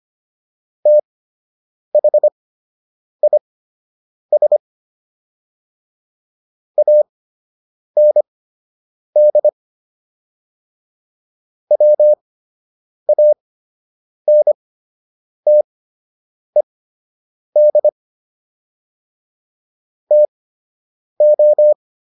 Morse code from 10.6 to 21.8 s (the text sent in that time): WANTED TO